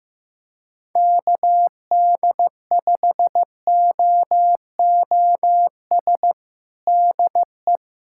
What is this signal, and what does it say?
Morse code, heard as a keyed tone, about 15 wpm, 700 Hz. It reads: KD5OOS DE
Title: USA Callsign DE Callsign 15wpm